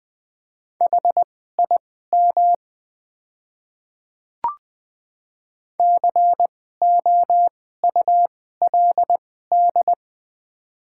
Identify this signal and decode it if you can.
Morse code, heard as a keyed tone, about 20 wpm, spaced as for 15 wpm, 700 Hz. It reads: HIM COULD
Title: 2x Character Spacing - Top 100 Words - 20wpm